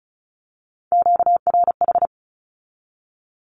Morse code, heard as a keyed tone, about 34 wpm, 700 Hz. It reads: QRH
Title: Q-Codes with Spoken Meaning - 35wpm